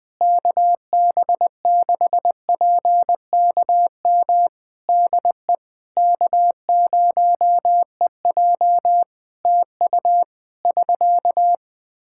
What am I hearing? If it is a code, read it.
Morse code, heard as a keyed tone, about 20 wpm, 700 Hz. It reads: KB6PKM DE K0EJ TU <SK>
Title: Practice QSO - Exchange 4 - Ending - 20wpm